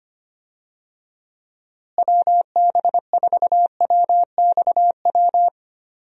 Morse code, heard as a keyed tone, about 25 wpm, 700 Hz. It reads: WB4WXW
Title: USA Call Signs - 6 Characters - Repeated 2x - 25wpm